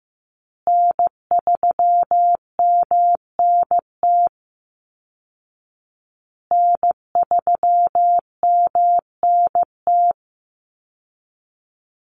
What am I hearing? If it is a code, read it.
Morse code, heard as a keyed tone, about 15 wpm, 700 Hz. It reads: N3MNT N3MNT
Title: Call Signs Repeated 2x - 15wpm